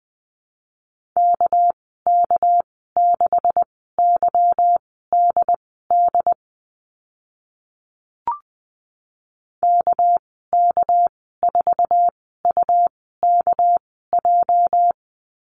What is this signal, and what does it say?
Morse code, heard as a keyed tone, about 20 wpm, spaced as for 15 wpm, 700 Hz. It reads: KK6YDD KK4UKJ
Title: 245 Lesson 39g - Letter z - Call Signs 20wpm